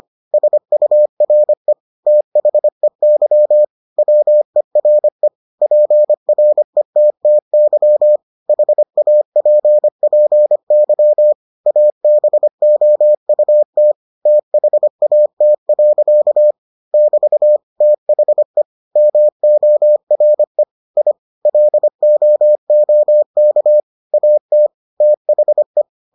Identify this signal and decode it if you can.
Morse code, heard as a keyed tone, about 25 wpm, 600 Hz. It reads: SURE THEY WERE PRETTY HAPPY ABOUT THAT. <BT> THE MORE I LOOK AT THE